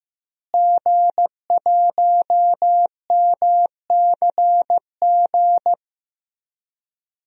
Morse code, heard as a keyed tone, about 15 wpm, 700 Hz. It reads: G1MCG
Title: DXCC-100 Call Signs - 5 Characters - Repeated 3x - 15wpm